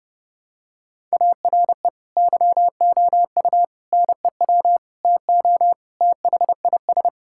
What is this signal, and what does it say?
Morse code, heard as a keyed tone, about 30 wpm, 700 Hz. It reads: ARE YOU NEW TO THIS
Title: Sentences from Top 1000 Words - Review Part 6 - 30wpm